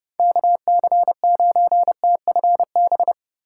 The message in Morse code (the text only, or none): KC9TFB